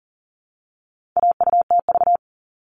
Morse code, heard as a keyed tone, about 40 wpm, 700 Hz. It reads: AUTV
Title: All Cut Numbers - 4-Digit 40wpm